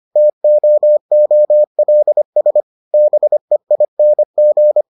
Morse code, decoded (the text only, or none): TOOLS BEING